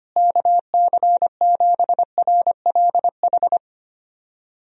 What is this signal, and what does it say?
Morse code, heard as a keyed tone, about 25 wpm, 700 Hz. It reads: KC7RLH
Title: USA Call Signs - 6 Characters - Repeated 3x - 25wpm